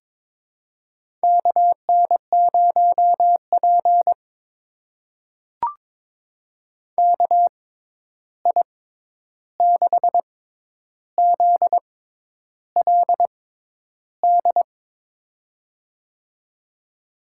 Morse code, heard as a keyed tone, about 21 wpm, spaced as for 10 wpm, 700 Hz. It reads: KN0P KI6ZLD
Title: USA Call Signs - 6x Character Spacing - 22wpm